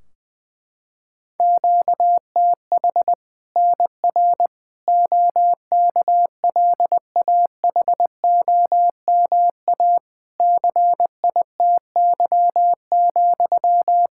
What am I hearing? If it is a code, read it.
Morse code, heard as a keyed tone, about 20 wpm, 700 Hz. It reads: QTH NR OKLAHOMA CITY,
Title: Practice QSO - Exchange 1 - Beginning - 20wpm